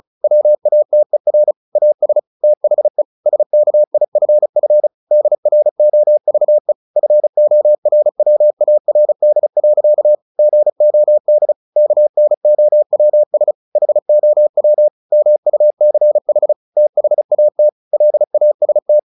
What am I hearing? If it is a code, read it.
Morse code, heard as a keyed tone, about 34 wpm, 600 Hz. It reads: WATER AS THE SKIFF DROVE FORWARD. GOD KNOWS HOW MUCH THAT LAST
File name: the old man and the sea 2 35wpm